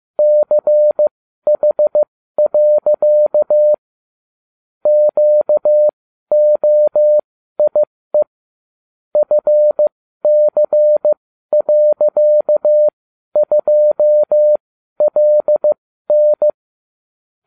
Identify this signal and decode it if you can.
Morse code, heard as a keyed tone, about 15 wpm, spaced as for 12 wpm, 600 Hz. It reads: CH. QOIE FC.2LN